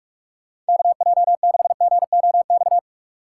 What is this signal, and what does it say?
Morse code, heard as a keyed tone, about 57 wpm, 700 Hz. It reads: KJ6GOX